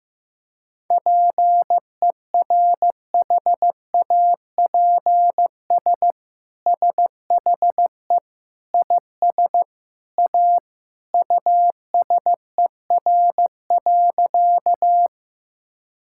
Morse code, heard as a keyed tone, about 15 wpm, 700 Hz. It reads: PERHAPS SHE IS A USER.
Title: Sentences from Top 1000 Words - Review Part 5 - 15wpm